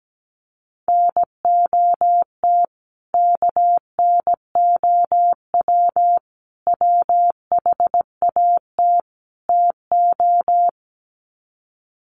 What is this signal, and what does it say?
Morse code, heard as a keyed tone, about 17 wpm, 700 Hz. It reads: NOT KNOW WHAT TO
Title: Sets of 4 Words 17wpm